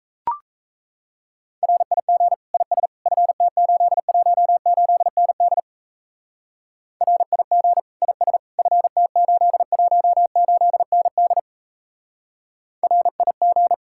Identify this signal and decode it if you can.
Morse code, heard as a keyed tone, about 35 wpm, 700 Hz. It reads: RIG IS FT818ND RIG IS FT818ND RIG